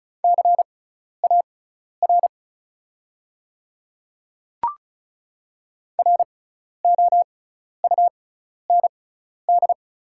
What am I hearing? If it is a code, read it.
Morse code, heard as a keyed tone, about 36 wpm, spaced as for 12 wpm, 690 Hz. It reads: CAR ROUND
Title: 6x Character Spacing - Top 300 Words - 35wpm